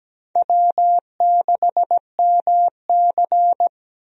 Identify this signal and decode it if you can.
Morse code, heard as a keyed tone, about 17 wpm, 700 Hz. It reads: W6MC
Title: USA Call Signs - 4 Characters - Repeated 2x - 17wpm